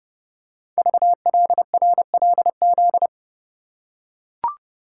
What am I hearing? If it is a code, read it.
Morse code, heard as a keyed tone, about 30 wpm, 700 Hz. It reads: VLRLZ
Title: Code Groups - 5 Letters - 30wpm